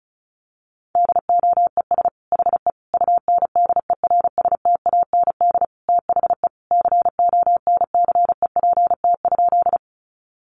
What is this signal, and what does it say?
Morse code, heard as a keyed tone, about 34 wpm, 700 Hz. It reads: DOES HE UNDERSTAND THE CONCEPT?
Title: Sentences from Top 1100 Words 35wpm